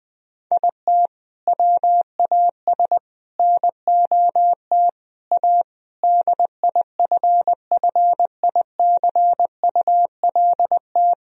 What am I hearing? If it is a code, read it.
Morse code, heard as a keyed tone, about 20 wpm, 700 Hz. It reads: IT WAS NOT A DIFFICULT
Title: Sentences from Top 1000 Words - Review Part 4 - 20wpm